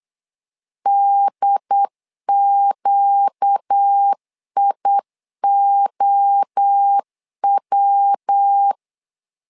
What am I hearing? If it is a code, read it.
Morse code, heard as a keyed tone, about 8 wpm, 800 Hz. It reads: DQIOW
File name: LEZIONI / 4 Q 8wpm 2732